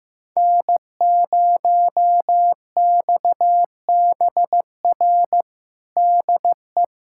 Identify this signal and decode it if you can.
Morse code, heard as a keyed tone, about 16 wpm, 700 Hz. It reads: N0XBR DE
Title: USA Callsign DE Callsign - 5 Characters - 15wpm